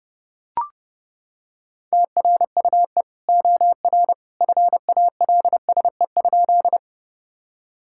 Morse code, heard as a keyed tone, about 32 wpm, 700 Hz. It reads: TRUE OR FALSE?
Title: Sentences from Top 1600 Words 30wpm